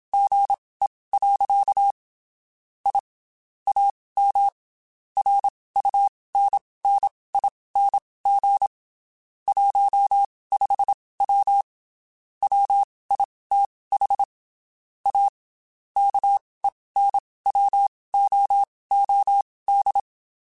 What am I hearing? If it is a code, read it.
Morse code, heard as a keyed tone, about 27 wpm, spaced as for 19 wpm, 800 Hz. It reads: GE. I AM RUNNING 15W WITH A KENWOOD
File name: QSOs / QSOs-mp3s / WD0FBT 1@25